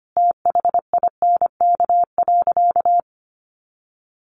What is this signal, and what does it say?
Morse code, heard as a keyed tone, about 25 wpm, 700 Hz. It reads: THINK.
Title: Sentences from Top 400 Words 25wpm